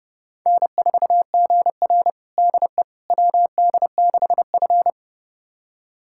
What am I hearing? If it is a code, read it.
Morse code, heard as a keyed tone, about 29 wpm, 700 Hz. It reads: N4GR DE WD6F
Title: USA Callsign DE Callsign - 4 Characters - 30wpm